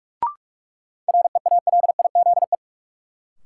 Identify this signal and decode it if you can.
Morse code, heard as a keyed tone, about 45 wpm, 700 Hz. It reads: REALIZE
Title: Sentences from Top 1000 Words - New Word Drill - 45wpm